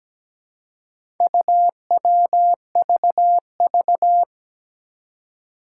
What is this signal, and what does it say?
Morse code, heard as a keyed tone, about 17 wpm, 700 Hz. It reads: UWVV
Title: Four Challenging Letter-Number Combinations 17wpm